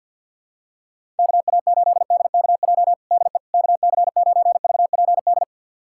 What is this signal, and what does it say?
Morse code, heard as a keyed tone, about 52 wpm, 700 Hz. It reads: KA8DKJ DE KC9VPD
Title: USA Callsign DE Callsign - 6 Characters - 50wpm